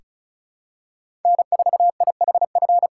Morse code, heard as a keyed tone, about 36 wpm, 700 Hz. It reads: N4IHF
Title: USA Callsign DE Callsign - 5 Characters - 35wpm